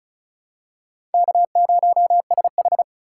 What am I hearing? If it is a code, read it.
Morse code, heard as a keyed tone, about 36 wpm, 700 Hz. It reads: K0SH
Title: USA Call Signs - 4 Characters - 8x Character Spacing - 35wpm